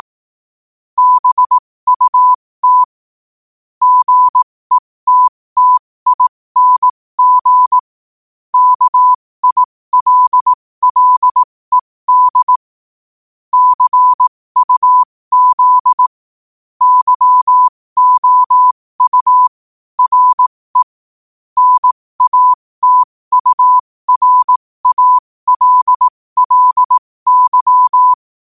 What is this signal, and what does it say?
Morse code, heard as a keyed tone, about 18 wpm, spaced as for 15 wpm, 1000 Hz. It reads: BUT GETTING KILLED CUZ YOU RE NATURALLY
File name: qotdfiles / 15wpm